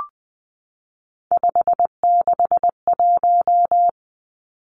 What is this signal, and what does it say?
Morse code, heard as a keyed tone, about 20 wpm, 700 Hz. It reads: 561
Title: Numbers - 3-Digit 20wpm